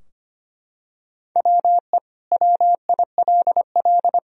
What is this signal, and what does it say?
Morse code, heard as a keyed tone, about 25 wpm, 700 Hz. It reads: WE WILL
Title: Sentences from Top 1000 Words - Review Part 4 - 25wpm